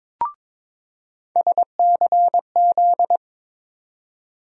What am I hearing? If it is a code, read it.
Morse code, heard as a keyed tone, about 22 wpm, 700 Hz. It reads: SCZ
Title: USA Call Sign Postfixes - 3 Letters 22wpm